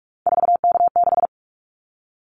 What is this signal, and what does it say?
Morse code, heard as a keyed tone, about 46 wpm, 700 Hz. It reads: 4K6
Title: Callsign Prefixes for Top 100 DXCC - 45wpm